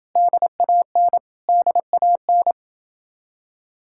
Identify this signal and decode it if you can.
Morse code, heard as a keyed tone, about 29 wpm, 700 Hz. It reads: DAN DAN